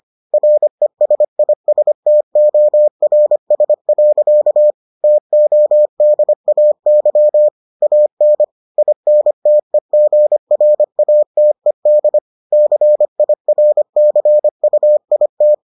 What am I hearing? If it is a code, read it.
Morse code, heard as a keyed tone, about 25 wpm, 600 Hz. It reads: RESISTORS. TODAY AN INTEGRATED CIRCUIT